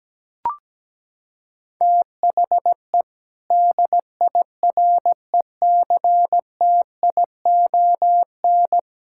Morse code, heard as a keyed tone, about 17 wpm, 700 Hz. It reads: THE DIRECTION
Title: Sentences from Top 1000 Words - Review Part 5 - 17wpm